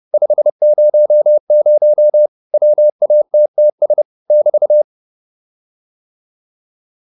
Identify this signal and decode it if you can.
Morse code, heard as a keyed tone, about 30 wpm, 600 Hz. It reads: 500 WATTS <BT>